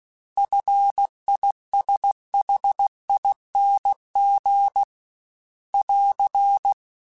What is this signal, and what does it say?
Morse code, heard as a keyed tone, about 16 wpm, 800 Hz. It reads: FISHING <AR>
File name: Short Runs / Short Runs-mp3s / SinglePrefixes @15